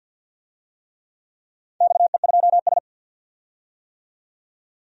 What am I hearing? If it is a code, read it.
Morse code, heard as a keyed tone, about 51 wpm, 700 Hz. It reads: XE2S
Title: DXCC-100 Call Signs - 4 Characters - 50wpm